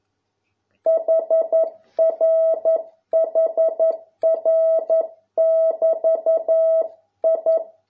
Morse code, heard as a keyed tone, about 11 wpm, 650 Hz. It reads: HR HR <BT> I